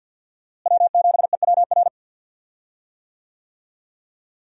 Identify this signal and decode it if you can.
Morse code, heard as a keyed tone, about 52 wpm, 700 Hz. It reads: W7EWR
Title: USA Call Signs - 5 Characters - Repeated 2x - 50wpm